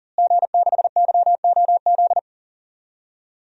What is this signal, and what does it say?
Morse code, heard as a keyed tone, about 40 wpm, 700 Hz. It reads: G6YOZ
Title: DXCC-100 Call Signs - 5 Characters - Repeated 2x - 40wpm